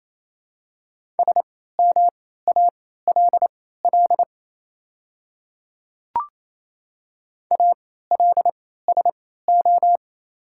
Morse code, heard as a keyed tone, about 28 wpm, spaced as for 16 wpm, 700 Hz. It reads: SMALL ALSO
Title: 3x Character Spacing - Top 300 Words - 28wpm